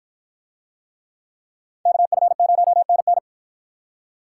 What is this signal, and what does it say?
Morse code, heard as a keyed tone, about 53 wpm, 700 Hz. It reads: KF0NN